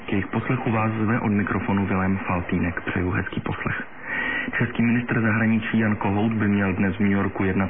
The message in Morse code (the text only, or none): none